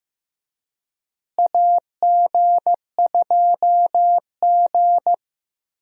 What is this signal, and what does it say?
Morse code, heard as a keyed tone, about 16 wpm, 700 Hz. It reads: AG2G
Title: USA Call Signs - 4 Characters - Repeated 3x - 15wpm